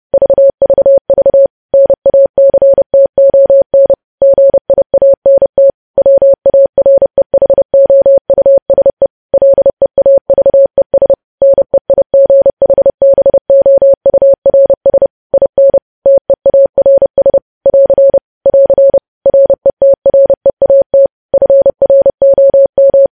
Morse code, heard as a keyed tone, about 30 wpm, 550 Hz. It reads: VVV NACTON GIANT WAREHOUSE LEAVES NEIGHBOURS IN TEARS <AR> <AR> RETREAT FROM
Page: news headlines in morse code at 30 wpm